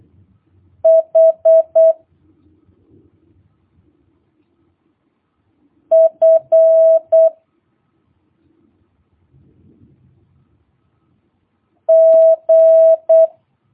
Morse code, heard as a keyed tone, about 8 wpm, 650 Hz. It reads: H F G
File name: jh8cbh / 2022cw